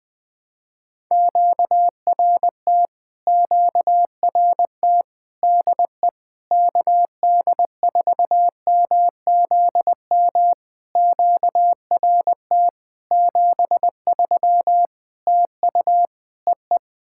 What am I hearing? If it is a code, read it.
Morse code, heard as a keyed tone, about 20 wpm, 700 Hz. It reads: QRT QRT DE KD4MZM QRT 73 TU EE
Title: POTA QSO Practice - 20wpm